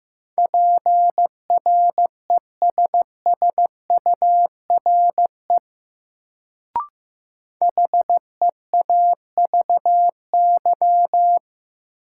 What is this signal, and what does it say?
Morse code, heard as a keyed tone, about 15 wpm, 700 Hz. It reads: PRESSURE HEAVY
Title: Sentences from Top 1000 Words - Review Part 6 - New Word Drill - 15wpm